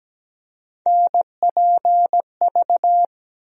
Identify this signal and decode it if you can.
Morse code, heard as a keyed tone, about 17 wpm, 700 Hz. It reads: NPV
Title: USA Call Sign Postfixes - 3 Letters 17wpm